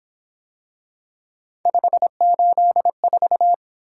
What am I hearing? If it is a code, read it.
Morse code, heard as a keyed tone, about 26 wpm, 700 Hz. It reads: 584
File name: Speed-Racing - Three Letter-Number Combinations - 22wpm